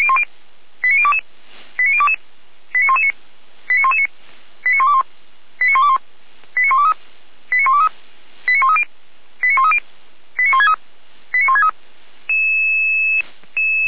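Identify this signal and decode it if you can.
Five-tone selective call (ZVEI), heard as quick runs of five tones, tones of about 70 ms. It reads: 89022 89022 89119 89119 89121 89121 89123 89123 89130 89130 89163 89163
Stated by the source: ZVEI five-tone call 89017 received by SDR.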